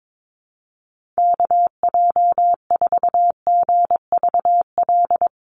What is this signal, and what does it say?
Morse code, heard as a keyed tone, about 22 wpm, 700 Hz. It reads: KJ4GVL
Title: USA Call Signs Repeated 3x 22wpm